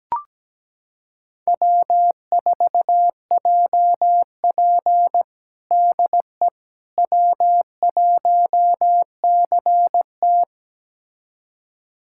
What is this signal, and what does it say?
Morse code, heard as a keyed tone, about 17 wpm, 700 Hz. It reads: W4JP DE W1CT
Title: USA Callsign DE Callsign - 4 Characters - 17wpm